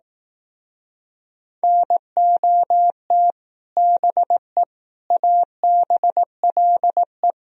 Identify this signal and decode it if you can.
Morse code, heard as a keyed tone, about 19 wpm, 700 Hz. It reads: NOT BE ABLE
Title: Speed-Racing - Sets of 4 Words 15wpm